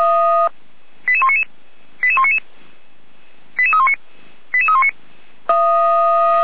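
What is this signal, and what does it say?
Five-tone selective call (ZVEI), heard as quick runs of five tones, tones of about 80 ms. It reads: 88199 88199 88319 88319
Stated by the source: ZVEI five-tone call received by SDR, archived by Feuerwehr Orlamünde.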